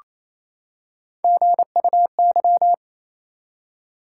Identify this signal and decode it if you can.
Morse code, heard as a keyed tone, about 28 wpm, 700 Hz. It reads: GUY
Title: Three Letter Words - 28wpm